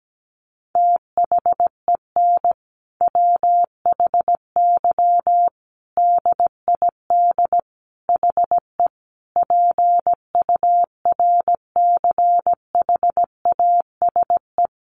Morse code, heard as a keyed tone, about 17 wpm, 700 Hz. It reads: THEN WHY DID HE PURCHASE
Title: Sentences from Top 800 Words 17wpm